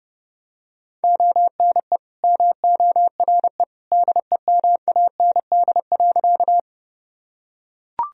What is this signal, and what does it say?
Morse code, heard as a keyed tone, about 32 wpm, 700 Hz. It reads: ONE MORE DEMAND.
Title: Sentences from Top 1000 Words - Review Part 4 - 30wpm